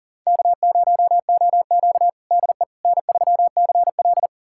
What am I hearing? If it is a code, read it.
Morse code, heard as a keyed tone, about 40 wpm, 700 Hz. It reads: K0OQ DE N3CL
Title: USA Callsign DE Callsign - 4 Characters - 40wpm